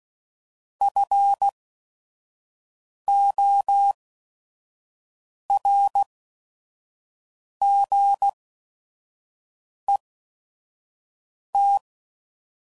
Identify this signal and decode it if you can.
Morse code, heard as a keyed tone, about 16 wpm, spaced as for 5 wpm, 800 Hz. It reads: FORGET
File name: CW Code Groups @5WPM / Technical Subjects / Gates Intro